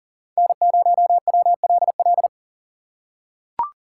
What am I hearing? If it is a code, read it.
Morse code, heard as a keyed tone, about 40 wpm, 700 Hz. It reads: N0WLL
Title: USA Call Signs - 5 Characters - 4x Character Spacing - 40wpm